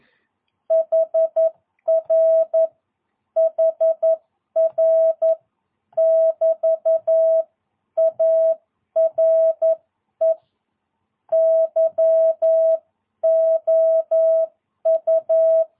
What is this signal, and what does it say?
Morse code, heard as a keyed tone, about 11 wpm, 660 Hz. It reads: HR HR <BT> ARE YOU
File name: jh8cbh / 2022cw